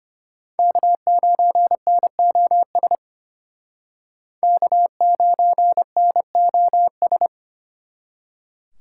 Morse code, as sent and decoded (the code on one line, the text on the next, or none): -.- ----. -. --- ... / -.- ----. -. --- ...
K9NOS K9NOS